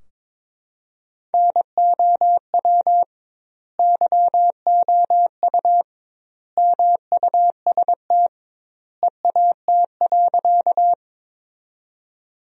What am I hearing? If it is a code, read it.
Morse code, heard as a keyed tone, about 22 wpm, 700 Hz. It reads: NOW YOU MUST EAT.
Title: 2x Word-Spacing - Sentences from Top 300 Words - 22wpm